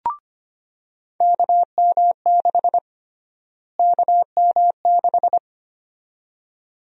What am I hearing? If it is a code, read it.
Morse code, heard as a keyed tone, about 25 wpm, 700 Hz. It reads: KM6 KM6
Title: USA Call Sign Prefixes - Repeated 2x - 25wpm